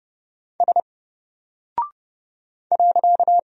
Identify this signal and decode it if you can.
Morse code, heard as a keyed tone, about 30 wpm, 700 Hz. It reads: S .
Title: Single Character (Letter:Number:Punctuation:Prosign) 30wpm